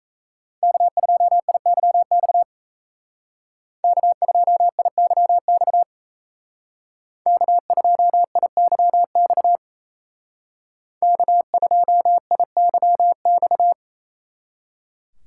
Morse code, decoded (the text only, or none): K2IYX K2IYX K2IYX K2IYX